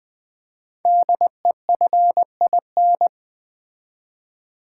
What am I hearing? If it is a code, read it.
Morse code, heard as a keyed tone, about 20 wpm, 700 Hz. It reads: DEFIN
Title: Build A Word - 5 Letters At A Time - 10-Letter Words - 20wpm